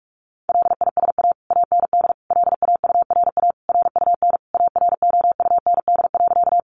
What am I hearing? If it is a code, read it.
Morse code, heard as a keyed tone, about 44 wpm, 700 Hz. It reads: LISA AND LAURA RUN AROUND.
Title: Additional Easy Phrases 45wpm